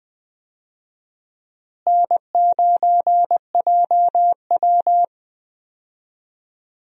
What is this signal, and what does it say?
Morse code, heard as a keyed tone, about 21 wpm, 700 Hz. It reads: N9JW
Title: USA Call Signs - 4 Characters - Repeated 3x - 20wpm